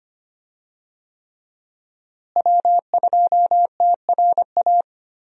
Morse code, heard as a keyed tone, about 25 wpm, 700 Hz. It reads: W2TRA
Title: USA Call Signs - 5 Characters - Repeated 3x - 25wpm